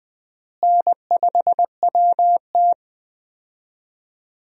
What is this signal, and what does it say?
Morse code, heard as a keyed tone, about 20 wpm, 700 Hz. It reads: N5WT
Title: USA Call Signs - 4 Characters - Repeated 2x - 20wpm